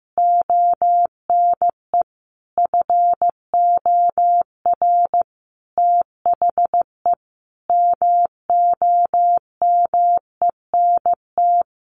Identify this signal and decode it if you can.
Morse code, heard as a keyed tone, about 15 wpm, 700 Hz. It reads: ONE FOR THE MOMENT
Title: Sentences from Top 1500 Words 15wpm